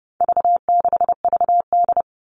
Morse code, heard as a keyed tone, about 30 wpm, 700 Hz. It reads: V6VD
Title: All Cut Numbers - 4-Digit 30wpm